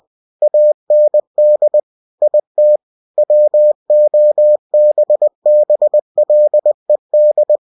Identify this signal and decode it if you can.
Morse code, heard as a keyed tone, about 20 wpm, 600 Hz. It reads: AND IT WOBBLED
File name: cw-books / winnie-the-pooh / winnie-20wpm-page-0100